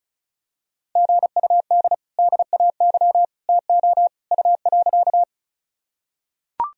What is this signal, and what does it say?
Morse code, heard as a keyed tone, about 34 wpm, 700 Hz. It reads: GUD DAY TO U.
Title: Sentences from Top 200 Words and Abbreviations 35wpm